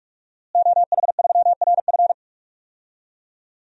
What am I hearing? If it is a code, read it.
Morse code, heard as a keyed tone, about 48 wpm, 700 Hz. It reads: OH3RF
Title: DXCC-100 Call Signs - 5 Characters - Repeated 2x - 45wpm